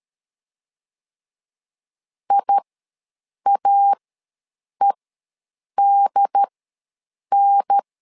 Morse code, heard as a keyed tone, about 13 wpm, spaced as for 7 wpm, 800 Hz. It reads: IAEDN
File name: LEZIONI / 3 MWRO / 3 MWRO ws 12wpm 1914